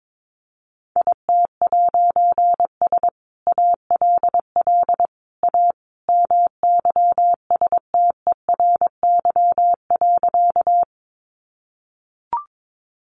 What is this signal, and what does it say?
Morse code, heard as a keyed tone, about 22 wpm, 700 Hz. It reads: IT'S ALL A MYSTERY.
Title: A Sound Of Thunder 22wpm